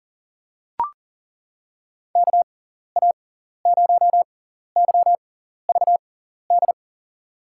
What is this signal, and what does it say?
Morse code, heard as a keyed tone, about 40 wpm, spaced as for 14 wpm, 700 Hz. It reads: KA0YVD
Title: USA Call Signs - 6 Characters - 6x Character Spacing - 40wpm